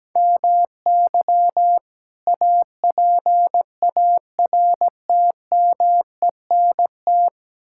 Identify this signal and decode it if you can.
Morse code, heard as a keyed tone, about 17 wpm, 700 Hz. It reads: MY APARTMENT
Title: Sentences from Top 1000 Words - Review Part 6 - 17wpm